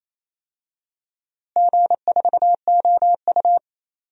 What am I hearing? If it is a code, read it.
Morse code, heard as a keyed tone, about 28 wpm, 700 Hz. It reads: G4OU
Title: DXCC-100 Call Signs - 4 Characters - Repeated 2x - 28wpm